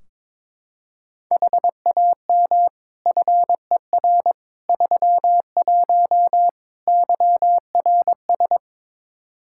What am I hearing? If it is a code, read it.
Morse code, heard as a keyed tone, about 22 wpm, 700 Hz. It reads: HAM FER 31 YRS
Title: K1USN SST QSO Practice 22wpm